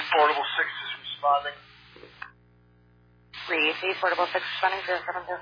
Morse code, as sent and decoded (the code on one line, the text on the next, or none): none